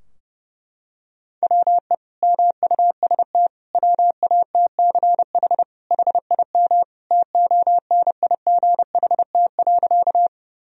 Morse code, heard as a keyed tone, about 30 wpm, 700 Hz. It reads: WE MUST WATCH HIM TONIGHT.
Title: Sentences from Top 1000 Words - Review Part 3 - 30wpm